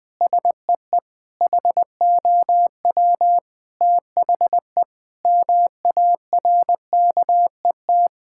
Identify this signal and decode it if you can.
Morse code, heard as a keyed tone, about 19 wpm, 700 Hz. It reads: SEE HOW THE MARKET
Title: Sentences with 8-Word Phrases - Spoken Beginning - 20wpm